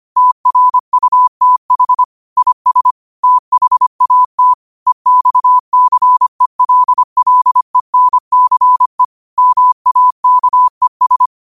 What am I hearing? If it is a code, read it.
Morse code, heard as a keyed tone, about 25 wpm, 1000 Hz. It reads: TRUTH IS THAT EXCELLENCE MAKES
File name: qotdfiles / 25wpm